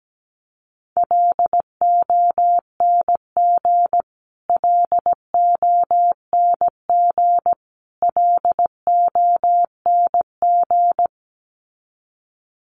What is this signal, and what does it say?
Morse code, heard as a keyed tone, about 17 wpm, 700 Hz. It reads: LONG LONG LONG
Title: Top 500 Words Repeated 17wpm